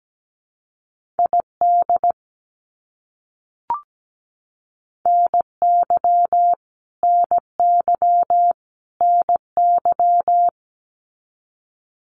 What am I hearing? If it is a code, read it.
Morse code, heard as a keyed tone, about 17 wpm, 700 Hz. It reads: ID NY NY NY
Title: US States Repeated 3x 17wpm